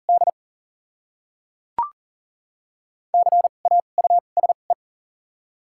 Morse code, heard as a keyed tone, about 40 wpm, spaced as for 11 wpm, 690 Hz. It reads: DCAUSE
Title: 2x Character Spacing - Top 500 Words - 40wpm